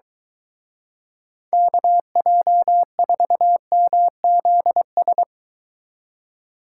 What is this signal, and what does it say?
Morse code, heard as a keyed tone, about 23 wpm, 700 Hz. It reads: KJ4MZS